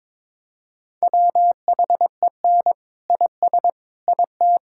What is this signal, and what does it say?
Morse code, heard as a keyed tone, about 21 wpm, 700 Hz. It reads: WHEN IS IT
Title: Sentences from Top 1000 Words - Review Part 5 - 22wpm